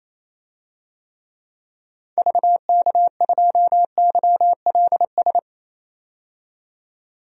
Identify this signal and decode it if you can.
Morse code, heard as a keyed tone, about 28 wpm, 700 Hz. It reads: VK2YLS
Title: DXCC-100 Call Signs - 6 Characters - 28wpm